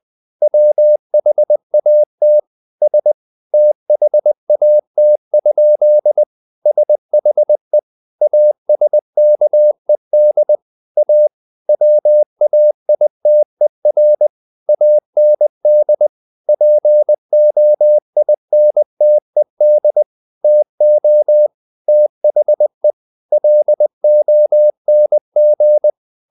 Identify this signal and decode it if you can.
Morse code, heard as a keyed tone, about 20 wpm, 600 Hz. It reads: WHAT S THAT? SHE ASKED A WAITER AND POINTED TO THE LONG
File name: the old man and the sea part 2 20wpm